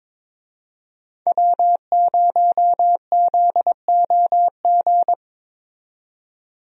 Morse code, heard as a keyed tone, about 21 wpm, 700 Hz. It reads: W0ZOG
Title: USA Call Signs - 5 Characters - 22wpm